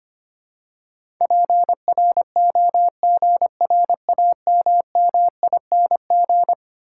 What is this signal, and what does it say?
Morse code, heard as a keyed tone, about 25 wpm, 700 Hz. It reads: PROGRAMMING
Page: morse-code-ninja-practice-25wpm